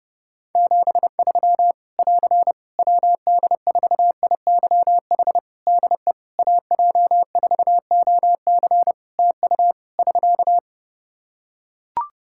Morse code, heard as a keyed tone, about 30 wpm, 700 Hz. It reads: 73 <AR> WD4IYH DE AJ4OC TU <SK>
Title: Practice QSO - Exchange 4 - Ending - 30wpm